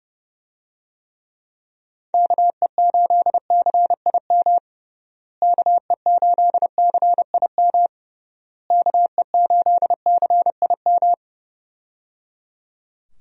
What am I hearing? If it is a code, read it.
Morse code, heard as a keyed tone, about 29 wpm, 700 Hz. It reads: KE8CIM KE8CIM KE8CIM
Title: USA Call Signs - 6 Characters - Repeated 3x - 30wpm